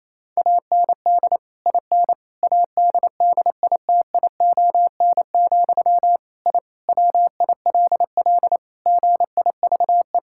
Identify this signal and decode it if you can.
Morse code, heard as a keyed tone, about 27 wpm, 700 Hz. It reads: AND IN ADDITION, I WILL GIVE
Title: Sentences from Top 1000 Words - Review Part 6 - 28wpm